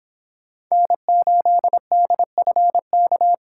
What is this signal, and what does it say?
Morse code, heard as a keyed tone, about 26 wpm, 700 Hz. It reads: N8DFK